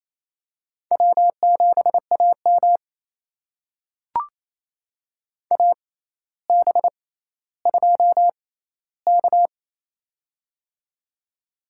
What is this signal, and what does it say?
Morse code, heard as a keyed tone, about 28 wpm, spaced as for 14 wpm, 700 Hz. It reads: W7AM AB2K
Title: USA Call Signs - 4 Characters - 6x Character Spacing - 28wpm